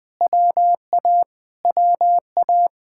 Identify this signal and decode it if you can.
Morse code, heard as a keyed tone, about 20 wpm, 700 Hz. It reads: WA WA
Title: Speed-Racing Common QSO Phrases Encore - 20wpm